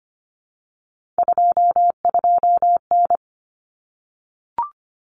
Morse code, heard as a keyed tone, about 25 wpm, 700 Hz. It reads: 22N
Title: Common Cut Numbers - 3-Digit 25wpm